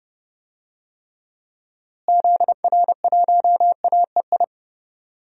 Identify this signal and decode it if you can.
Morse code, heard as a keyed tone, about 31 wpm, 700 Hz. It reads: ZR1AEI
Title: DXCC-100 Call Signs - 6 Characters - Repeated 3x - 30wpm